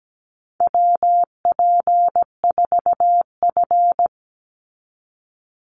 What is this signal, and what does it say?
Morse code, heard as a keyed tone, about 17 wpm, 700 Hz. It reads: WP4F
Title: Call Signs - 17wpm